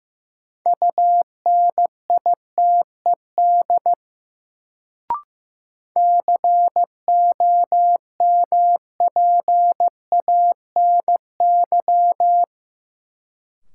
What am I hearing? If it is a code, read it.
Morse code, heard as a keyed tone, about 15 wpm, 700 Hz. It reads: UNITED COMPANY
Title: Sentences from Top 600 Words - New Word Drill - 15wpm